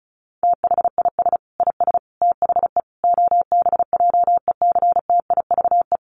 Morse code, heard as a keyed tone, about 34 wpm, 700 Hz. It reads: THIS IS THE OBJECTIVE
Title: Sentences from Top 1200 Words 35wpm